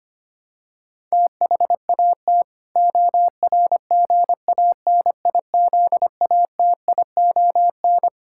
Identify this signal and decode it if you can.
Morse code, heard as a keyed tone, about 25 wpm, 700 Hz. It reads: THAT ORGANIZATION
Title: Sentences from Top 1000 Words - Review Part 6 - 25wpm